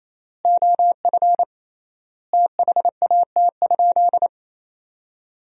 Morse code, heard as a keyed tone, about 28 wpm, 700 Hz. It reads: OF THAT?
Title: 3x Word-Spacing - Sentences from Top 100 Words - 28wpm